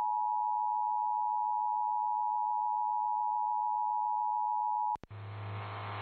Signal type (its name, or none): Emergency Alert System (SAME) tones